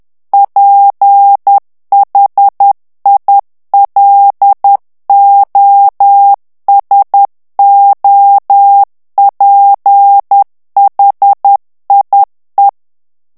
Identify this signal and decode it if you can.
Morse code, heard as a keyed tone, about 10 wpm, 800 Hz. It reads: PHILOSOPHIE
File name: morse mp3 / FRANCAIS TEXTE CLAIR 2 VITESSE 10 MOTS MINUTES